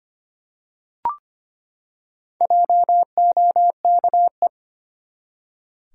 Morse code, heard as a keyed tone, about 25 wpm, 700 Hz. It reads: JOKE
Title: Four Letter Words - 25wpm